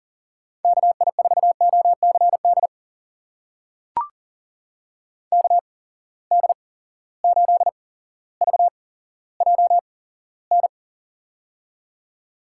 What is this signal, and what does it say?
Morse code, heard as a keyed tone, about 40 wpm, spaced as for 17 wpm, 700 Hz. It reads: KI4OCD KD8VJN
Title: USA Call Signs - 8x Character Spacing - 40wpm